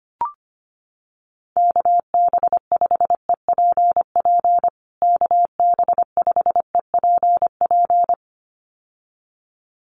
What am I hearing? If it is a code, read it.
Morse code, heard as a keyed tone, about 25 wpm, 700 Hz. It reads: KB5EPP KB5EPP
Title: USA Call Signs Repeated 2x 25wpm